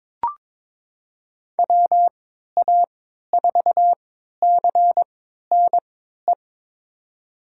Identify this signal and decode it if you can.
Morse code, heard as a keyed tone, about 22 wpm, spaced as for 13 wpm, 700 Hz. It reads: WA4CNE
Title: USA Call Signs - 6 Characters - 3x Character Spacing - 22wpm